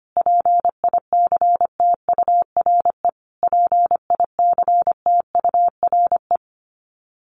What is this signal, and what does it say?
Morse code, heard as a keyed tone, about 25 wpm, 700 Hz. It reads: PICTURE PICTURE
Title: Top 200 Words Repeated 25wpm